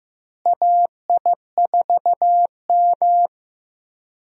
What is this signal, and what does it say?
Morse code, heard as a keyed tone, about 16 wpm, 700 Hz. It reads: AI4M
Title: USA Call Signs - 4 Characters - Repeated 3x - 15wpm